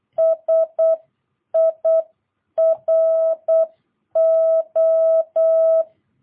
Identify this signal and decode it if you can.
Morse code, heard as a keyed tone, about 8 wpm, 650 Hz. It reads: SIRO
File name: jh8cbh / 2022cw